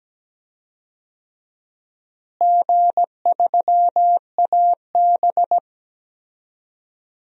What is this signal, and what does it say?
Morse code, heard as a keyed tone, about 17 wpm, 700 Hz. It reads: G3AB
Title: DXCC-100 Call Signs - 4 Characters - Repeated 3x - 17wpm